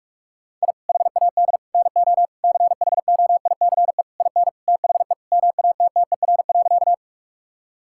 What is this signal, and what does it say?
Morse code, heard as a keyed tone, about 41 wpm, 700 Hz. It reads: I HAD NO CHOICE IN THE MATTER.
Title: Sentences from Top 1000 Words - Review Part 6 - 45wpm